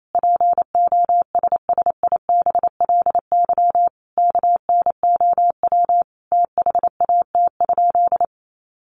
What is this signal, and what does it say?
Morse code, heard as a keyed tone, about 28 wpm, 700 Hz. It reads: POSSIBLY KNOW THAT?
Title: Sentences from Top 1300 Words 28wpm